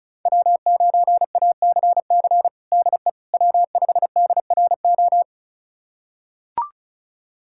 Morse code, heard as a keyed tone, about 36 wpm, 700 Hz. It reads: W9ACC DE W5DRO
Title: USA Callsign DE Callsign - 5 Characters - 35wpm